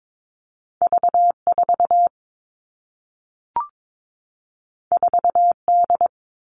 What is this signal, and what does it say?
Morse code, heard as a keyed tone, about 22 wpm, 700 Hz. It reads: V4 4D
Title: All Cut Numbers - 2-Digit 22wpm